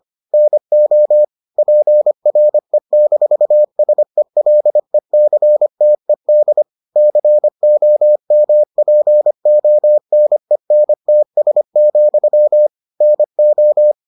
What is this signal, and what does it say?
Morse code, heard as a keyed tone, about 25 wpm, 600 Hz. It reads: NO PRE-SELECTED COMPONENTS, NO